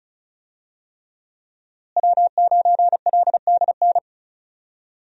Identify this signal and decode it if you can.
Morse code, heard as a keyed tone, about 36 wpm, 700 Hz. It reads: W9LDN
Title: USA Call Signs - 5 Characters - Repeated 3x - 35wpm